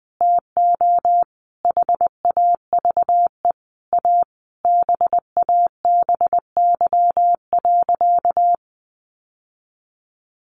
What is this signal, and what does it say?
Morse code, heard as a keyed tone, about 20 wpm, 700 Hz. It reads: TO HAVE A BABY.
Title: Sentences from Top 1300 Words 20wpm